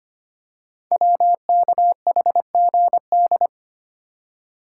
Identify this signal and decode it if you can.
Morse code, heard as a keyed tone, about 25 wpm, 700 Hz. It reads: WKHGD
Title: Code Groups - 5 Letters - 25wpm